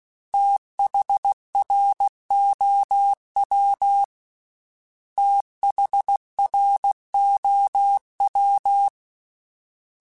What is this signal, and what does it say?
Morse code, heard as a keyed tone, about 16 wpm, 800 Hz. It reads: THROW THROW
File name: THs / TH / TH Prefixes 3x2 @15